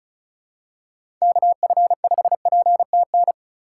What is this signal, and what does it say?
Morse code, heard as a keyed tone, about 36 wpm, 700 Hz. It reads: KF5PTN
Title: USA Call Signs - 6 Characters - 3x Character Spacing - 35wpm